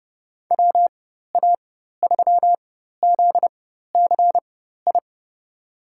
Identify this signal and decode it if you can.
Morse code, heard as a keyed tone, about 32 wpm, spaced as for 14 wpm, 700 Hz. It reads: WA3ZCI
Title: USA Call Signs - 6 Characters - 4x Character Spacing - 30wpm